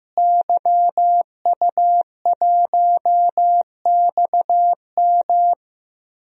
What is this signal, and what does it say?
Morse code, heard as a keyed tone, about 15 wpm, 700 Hz. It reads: YU1XM
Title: DXCC-100 Call Signs - 5 Characters - Repeated 3x - 15wpm